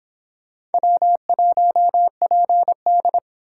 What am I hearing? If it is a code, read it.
Morse code, heard as a keyed tone, about 31 wpm, 700 Hz. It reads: W1PD